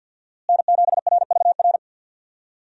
Morse code, heard as a keyed tone, about 51 wpm, 700 Hz. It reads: N7RVR
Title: USA Call Signs - 5 Characters - Repeated 2x - 50wpm